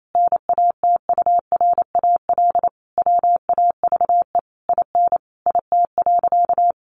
Morse code, heard as a keyed tone, about 28 wpm, 700 Hz. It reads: NATURAL WAVE IN IT.
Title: Sentences from Top 1400 Words 28wpm